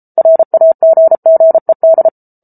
Morse code, heard as a keyed tone, about 34 wpm, 650 Hz. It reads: RAGGED